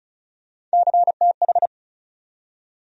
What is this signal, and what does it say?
Morse code, heard as a keyed tone, about 36 wpm, 700 Hz. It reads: CTH
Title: USA Call Sign Postfixes - 3 Letters 35wpm